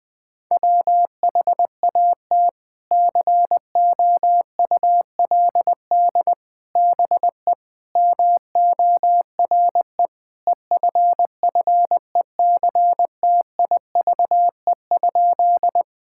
Morse code, heard as a keyed tone, about 21 wpm, 700 Hz. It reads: WHAT COULD BE MORE EFFECTIVE?
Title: Sentences from Top 1000 Words - Review Part 6 - 20wpm